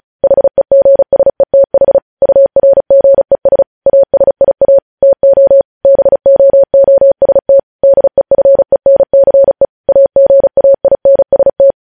Morse code, heard as a keyed tone, about 36 wpm, 550 Hz. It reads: HEGSETH URGES ASIA TO BOOST DEFENCE AGAINST
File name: podcast / News headlines-1x-35x35